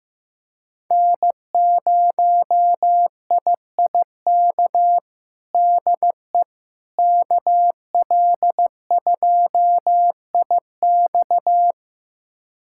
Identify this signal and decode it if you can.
Morse code, heard as a keyed tone, about 15 wpm, 700 Hz. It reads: N0IIK DE KL2IX
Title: USA Callsign DE Callsign - 5 Characters - 15wpm